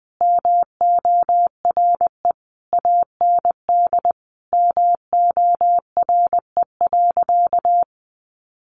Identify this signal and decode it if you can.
Morse code, heard as a keyed tone, about 20 wpm, 700 Hz. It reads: MORE AND MORE.
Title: Sentences from Top 500 Words 20wpm